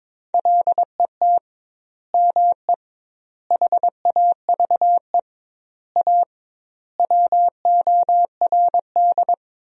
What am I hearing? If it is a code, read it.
Morse code, heard as a keyed tone, about 22 wpm, 700 Hz. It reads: LET ME HAVE A WORD
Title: 2x Word-Spacing - Sentences from Top 300 Words - 22wpm